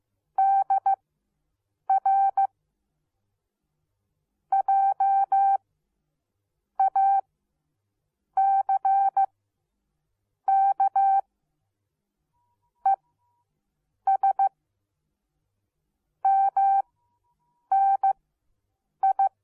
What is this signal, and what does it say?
Morse code, heard as a keyed tone, about 15 wpm, spaced as for 6 wpm, 780 Hz. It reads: DR JACK ES MNI